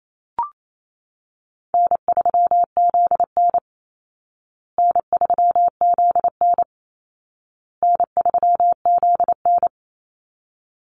Morse code, heard as a keyed tone, about 28 wpm, 700 Hz. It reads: N3ZN N3ZN N3ZN
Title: Call Signs Repeated 2x - 28wpm